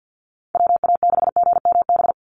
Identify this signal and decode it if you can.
Morse code, heard as a keyed tone, about 50 wpm, 700 Hz. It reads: WA6GMB
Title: USA Call Signs 50wpm